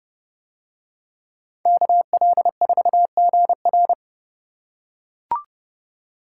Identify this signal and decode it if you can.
Morse code, heard as a keyed tone, about 31 wpm, 700 Hz. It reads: KL4GR